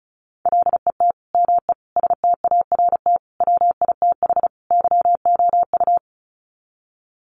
Morse code, heard as a keyed tone, about 34 wpm, 700 Hz. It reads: LET ME START WITH YOU
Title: Sets of 5 Words Encore 35wpm